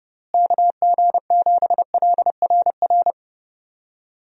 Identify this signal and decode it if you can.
Morse code, heard as a keyed tone, about 30 wpm, 700 Hz. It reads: KG7LRR
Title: USA Call Signs - 6 Characters - Repeated 3x - 30wpm